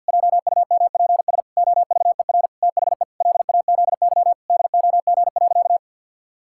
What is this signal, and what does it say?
Morse code, heard as a keyed tone, about 52 wpm, 700 Hz. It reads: JUMPS OVER THE LAZY DOG.
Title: Pangram Challenge v2 - 50wpm